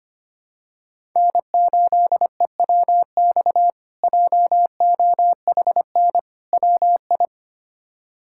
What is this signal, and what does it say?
Morse code, heard as a keyed tone, about 25 wpm, 700 Hz. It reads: N8EWX JOHN WI
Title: K1USN SST QSO Practice - 25wpm